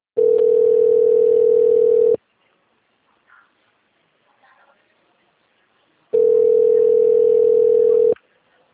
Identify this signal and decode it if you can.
Ringback tone (North America), 440 + 480 Hz.